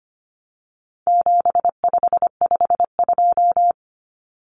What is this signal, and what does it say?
Morse code, heard as a keyed tone, about 25 wpm, 700 Hz. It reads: 7552
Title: Common Cut Numbers - 4-Digit 25wpm